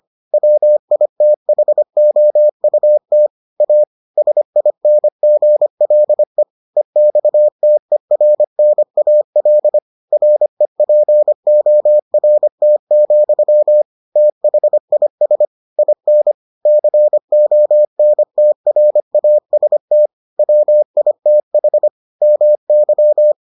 Morse code, heard as a keyed tone, about 25 wpm, 600 Hz. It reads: WITHOUT A SINGLE EXTERNAL REPORT, THIS IN CONTRAST WITH MY